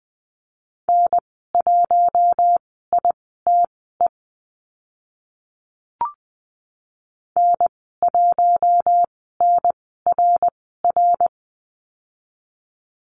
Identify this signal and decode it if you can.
Morse code, heard as a keyed tone, about 20 wpm, spaced as for 14 wpm, 700 Hz. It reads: N1ITE N1NRR
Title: 069 Lesson 12g - Letter d - Call Signs 20wpm